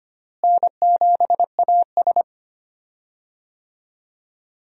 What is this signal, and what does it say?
Morse code, heard as a keyed tone, about 25 wpm, 700 Hz. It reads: N7AS
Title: USA Call Signs - 4 Characters - Repeated 3x - 25wpm